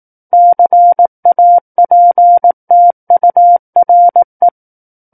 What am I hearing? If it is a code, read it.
Morse code, heard as a keyed tone, about 18 wpm, 700 Hz. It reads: CAPTURE
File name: 18 WPM